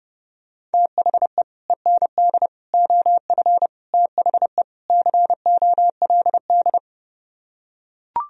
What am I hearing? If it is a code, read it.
Morse code, heard as a keyed tone, about 30 wpm, 700 Hz. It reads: THE END OF THE COLD